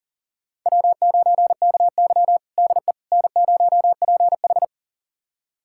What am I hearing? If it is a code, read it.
Morse code, heard as a keyed tone, about 40 wpm, 700 Hz. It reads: W9KY DE N0PH
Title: USA Callsign DE Callsign - 4 Characters - 40wpm